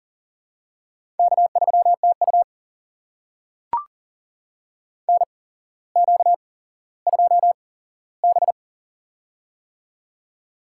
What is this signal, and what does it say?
Morse code, heard as a keyed tone, about 37 wpm, spaced as for 17 wpm, 700 Hz. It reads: K3TU NQ2B
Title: USA Call Signs - 8x Character Spacing - 40wpm